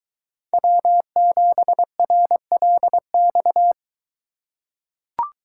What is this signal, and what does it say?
Morse code, heard as a keyed tone, about 23 wpm, 700 Hz. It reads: W7RLX